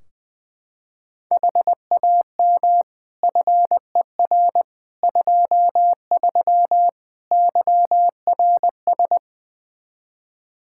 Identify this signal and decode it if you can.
Morse code, heard as a keyed tone, about 20 wpm, 700 Hz. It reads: HAM FER 23 YRS
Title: Practice QSO - Exchange 3 - More Info - 20wpm